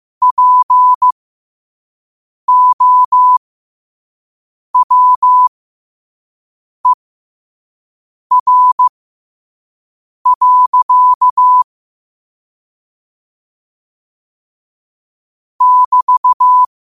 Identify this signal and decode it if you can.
Morse code, heard as a keyed tone, about 15 wpm, spaced as for 5 wpm, 1000 Hz. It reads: POWER. <BT>
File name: qotdfiles / 5wpm